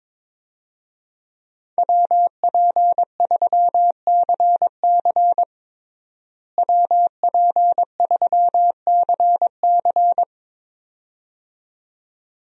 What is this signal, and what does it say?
Morse code, heard as a keyed tone, about 22 wpm, 700 Hz. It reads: WP3CC WP3CC
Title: USA Call Signs - 5 Characters - Repeated 2x - 22wpm